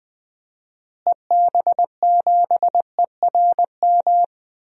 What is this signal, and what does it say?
Morse code, heard as a keyed tone, about 20 wpm, 700 Hz. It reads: EB7ERM
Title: DXCC-100 Call Signs - 6 Characters - Repeated 2x - 20wpm